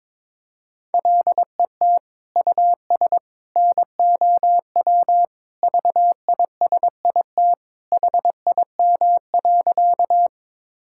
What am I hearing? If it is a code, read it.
Morse code, heard as a keyed tone, about 22 wpm, 700 Hz. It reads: LET US NOW VISIT HIM.
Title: Sentences from Top 1000 Words - Review Part 6 - 22wpm